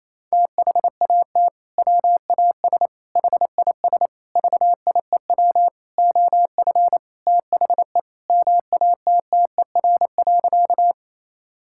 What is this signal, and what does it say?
Morse code, heard as a keyed tone, about 28 wpm, 700 Hz. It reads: THAT WAS HIS VIEW OF THE MATTER.
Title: Sentences from Top 1000 Words - Review Part 6 - 28wpm